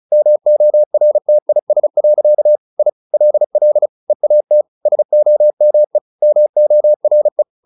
Morse code, heard as a keyed tone, about 35 wpm, 600 Hz. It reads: MORTIS. I LL EAT SOME MORE
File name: the old man and the sea / the old man and the sea 1 35wpm